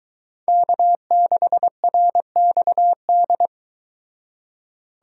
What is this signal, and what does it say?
Morse code, heard as a keyed tone, about 23 wpm, 700 Hz. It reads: K6RXD